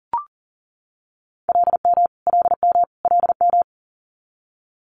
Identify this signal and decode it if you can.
Morse code, heard as a keyed tone, about 42 wpm, 700 Hz. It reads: LM LM LM
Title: Multi-letter Phonemes Repeated 3x 40wpm